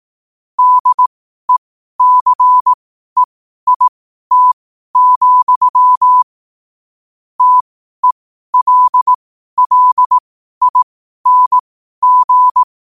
Morse code, heard as a keyed tone, about 18 wpm, spaced as for 12 wpm, 1000 Hz. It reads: DECEIT, TELLING
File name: qotdfiles / 12wpm